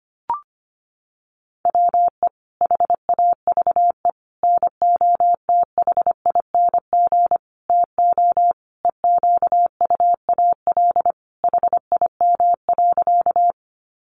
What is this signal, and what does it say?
Morse code, heard as a keyed tone, about 25 wpm, 700 Hz. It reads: WE HAVE NOTHING TO EQUAL HIM.
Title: Sentences from Top 1400 Words 25wpm